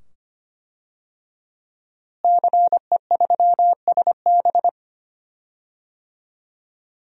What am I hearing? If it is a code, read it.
Morse code, heard as a keyed tone, about 25 wpm, 700 Hz. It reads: CE3SB
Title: DXCC-100 Call Signs - 5 Characters - Repeated 2x - 25wpm